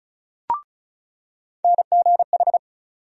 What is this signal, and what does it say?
Morse code, heard as a keyed tone, about 36 wpm, 700 Hz. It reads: NGH
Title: USA Call Sign Postfixes - 3 Letters 35wpm